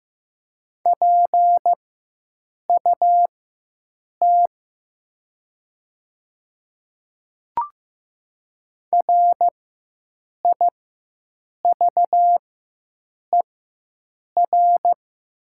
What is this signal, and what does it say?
Morse code, heard as a keyed tone, about 15 wpm, spaced as for 7 wpm, 700 Hz. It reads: PUT RIVER
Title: 4x Character Spacing - Top 300 Words - 15wpm